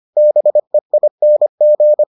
Morse code, heard as a keyed tone, about 25 wpm, 600 Hz. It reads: BEING